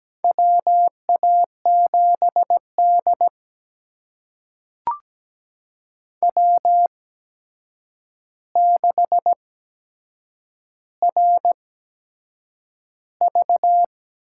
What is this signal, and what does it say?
Morse code, heard as a keyed tone, about 17 wpm, spaced as for 7 wpm, 700 Hz. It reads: WA7D W6RV
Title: USA Call Signs - 4 Characters - 8x Character Spacing - 17wpm